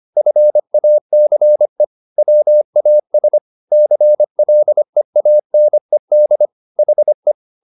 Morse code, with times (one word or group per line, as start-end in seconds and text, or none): 0.2-1.8 FACE
2.2-3.4 WAS
3.7-6.5 CLEANED
6.8-7.3 HE